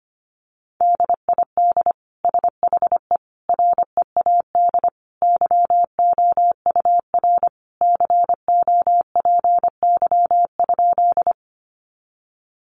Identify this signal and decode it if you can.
Morse code, heard as a keyed tone, about 25 wpm, 700 Hz. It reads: DID SHE READ YOUR COPY?
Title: Sentences from Top 1400 Words 25wpm